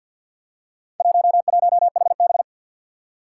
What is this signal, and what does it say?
Morse code, heard as a keyed tone, about 48 wpm, 700 Hz. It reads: 11HB
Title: Four Challenging Letter-Number Combinations 50wpm